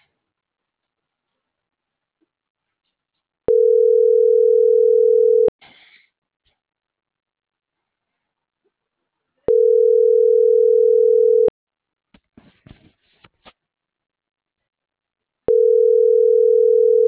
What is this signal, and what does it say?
Ringback tone (North America), 440 + 480 Hz.